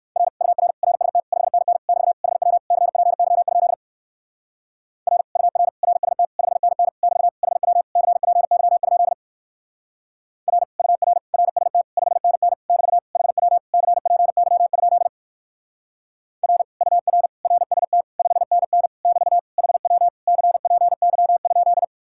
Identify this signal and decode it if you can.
Morse code, heard as a keyed tone, about 55 wpm, 700 Hz. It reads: R UR RST 5NN <BT> HW CPY? R UR RST 5NN <BT> HW CPY? R UR RST 5NN <BT> HW CPY? R UR RST 5NN <BT> HW CPY?
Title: Speed-Racing Common QSO Phrases - 45wpm